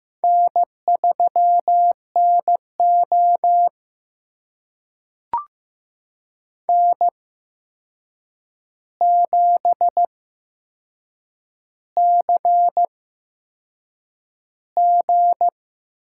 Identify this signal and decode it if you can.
Morse code, heard as a keyed tone, about 15 wpm, spaced as for 6 wpm, 700 Hz. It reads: N3NO N7CG
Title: USA Call Signs - 4 Characters - 8x Character Spacing - 15wpm